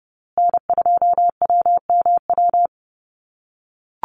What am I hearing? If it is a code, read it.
Morse code, heard as a keyed tone, about 30 wpm, 700 Hz. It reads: N2WMW
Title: USA Call Signs Repeated 2x 30wpm